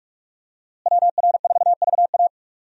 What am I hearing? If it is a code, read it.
Morse code, heard as a keyed tone, about 46 wpm, 700 Hz. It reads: WR4VA
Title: USA Call Signs - 5 Characters - Repeated 2x - 45wpm